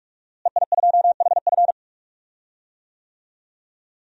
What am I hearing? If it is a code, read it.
Morse code, heard as a keyed tone, about 46 wpm, 700 Hz. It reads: EI2HF
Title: DXCC-100 Call Signs - 5 Characters - Repeated 2x - 45wpm